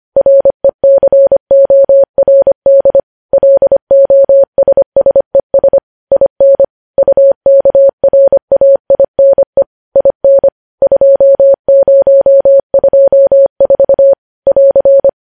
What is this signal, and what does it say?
Morse code, heard as a keyed tone, about 25 wpm, 550 Hz. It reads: RECORD LOSSES IN UKRAINE IN 2024 <AR>
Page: news-headlines-in-morse-code-at-25-wpm